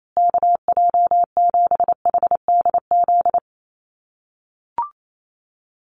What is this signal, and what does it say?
Morse code, heard as a keyed tone, about 28 wpm, 700 Hz. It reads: KJ7HDZ
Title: USA Call Signs Repeated 3x 28wpm